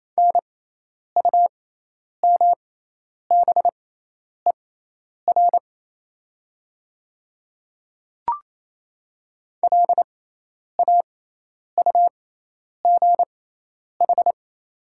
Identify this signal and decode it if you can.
Morse code, heard as a keyed tone, about 28 wpm, spaced as for 10 wpm, 700 Hz. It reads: NUMBER LAUGH
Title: 6x Character Spacing - Top 500 Words - 28wpm